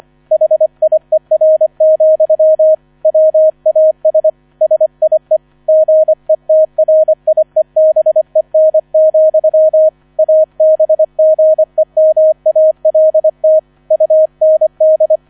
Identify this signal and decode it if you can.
Morse code, heard as a keyed tone, about 25 wpm, 630 Hz. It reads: HIER, WAS SIE GETRIEBEN, ABGEMALT UND